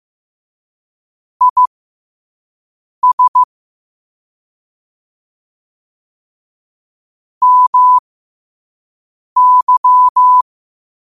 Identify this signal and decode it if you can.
Morse code, heard as a keyed tone, about 15 wpm, spaced as for 5 wpm, 1000 Hz. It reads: IS MY
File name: qotdfiles / 5wpm